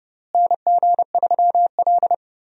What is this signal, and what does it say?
Morse code, heard as a keyed tone, about 31 wpm, 700 Hz. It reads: NG3L
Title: USA Call Signs - 4 Characters - Repeated 2x - 30wpm